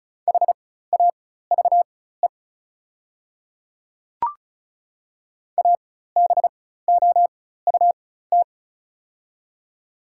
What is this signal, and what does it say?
Morse code, heard as a keyed tone, about 36 wpm, spaced as for 16 wpm, 700 Hz. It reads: HAVE ABOUT
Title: 4x Character Spacing - Top 100 Words - 35wpm